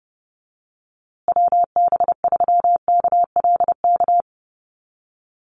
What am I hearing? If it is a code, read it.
Morse code, heard as a keyed tone, about 30 wpm, 700 Hz. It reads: WB3KLK
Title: USA Call Signs Repeated 3x 30wpm